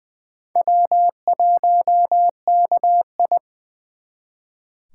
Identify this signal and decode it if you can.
Morse code, heard as a keyed tone, about 25 wpm, 700 Hz. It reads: W1KI
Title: Speed Racing - USA 1x2 Call Signs 20wpm